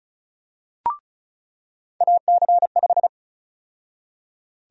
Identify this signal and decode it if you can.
Morse code, heard as a keyed tone, about 34 wpm, 700 Hz. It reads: AC5
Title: USA Call Sign Prefixes - 35wpm